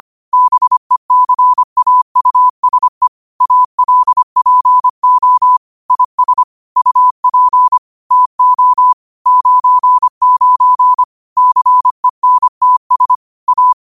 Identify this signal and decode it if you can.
Morse code, heard as a keyed tone, about 25 wpm, 1000 Hz. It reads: BECAUSE ALPO IS UP TO 99 CENTS A
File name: qotdfiles / 25wpm